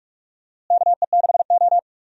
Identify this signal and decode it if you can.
Morse code, heard as a keyed tone, about 41 wpm, 700 Hz. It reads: KE6O
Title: USA Call Signs - 4 Characters - Repeated 3x - 45wpm